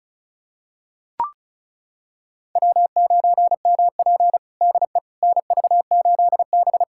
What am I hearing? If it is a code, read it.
Morse code, heard as a keyed tone, about 36 wpm, 700 Hz. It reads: W9MP DE NV8B
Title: USA Callsign DE Callsign - 4 Characters - 35wpm